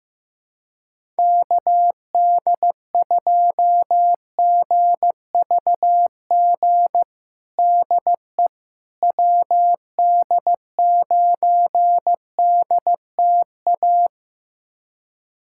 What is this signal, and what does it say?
Morse code, heard as a keyed tone, about 15 wpm, 700 Hz. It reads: KD2GVG DE WD9DTA
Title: USA Callsign DE Callsign - 6 Characters - 15wpm